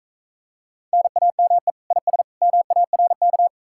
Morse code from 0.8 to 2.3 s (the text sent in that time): NAME IS